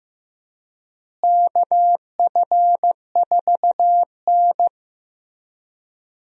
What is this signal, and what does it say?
Morse code, heard as a keyed tone, about 15 wpm, 700 Hz. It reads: KF4N
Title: USA Call Signs - 4 Characters - Repeated 2x - 15wpm